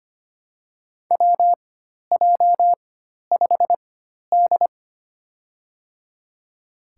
Morse code, heard as a keyed tone, about 25 wpm, spaced as for 12 wpm, 700 Hz. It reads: WJ5D
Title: USA Call Signs - 4 Characters - 4x Character Spacing - 25wpm